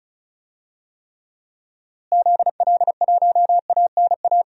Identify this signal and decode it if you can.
Morse code, heard as a keyed tone, about 34 wpm, 700 Hz. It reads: ZL1ANA
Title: DXCC-100 Call Signs - 6 Characters - Repeated 3x - 35wpm